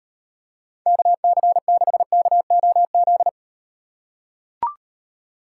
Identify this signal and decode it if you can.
Morse code, heard as a keyed tone, about 38 wpm, 700 Hz. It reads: KC6KOZ